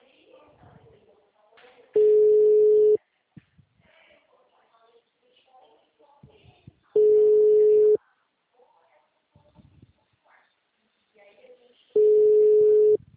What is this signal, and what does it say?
Ringback tone (Europe, 425 Hz), with speech.
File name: out- -2029- - - .12